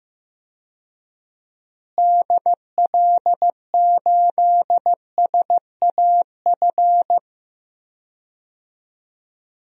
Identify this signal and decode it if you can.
Morse code, heard as a keyed tone, about 16 wpm, 700 Hz. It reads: DL8SAF
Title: DXCC-100 Call Signs - 6 Characters - Repeated 3x - 15wpm